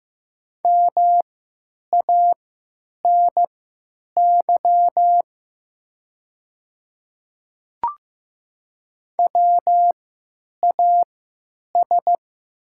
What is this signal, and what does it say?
Morse code, heard as a keyed tone, about 15 wpm, spaced as for 9 wpm, 700 Hz. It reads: MANY WAS
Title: 3x Character Spacing - Top 100 Words - 15wpm